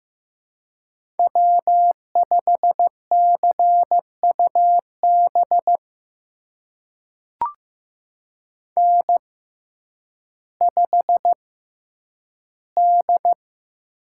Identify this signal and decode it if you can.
Morse code, heard as a keyed tone, about 15 wpm, spaced as for 9 wpm, 700 Hz. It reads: W5CUB N5D
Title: USA Call Signs - 5 Characters - 6x Character Spacing - 15wpm